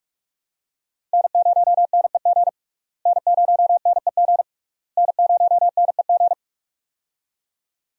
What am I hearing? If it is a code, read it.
Morse code, heard as a keyed tone, about 41 wpm, 700 Hz. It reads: N0NEG N0NEG N0NEG
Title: USA Call Signs - 5 Characters - Repeated 3x - 45wpm